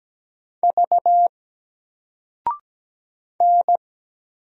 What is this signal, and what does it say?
Morse code, heard as a keyed tone, about 17 wpm, 700 Hz. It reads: V N
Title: Single Letter-Number 17wpm